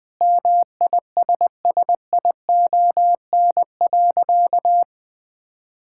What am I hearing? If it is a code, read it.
Morse code, heard as a keyed tone, about 20 wpm, 700 Hz. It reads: MISSION.
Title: Sentences from Top 1600 Words 20wpm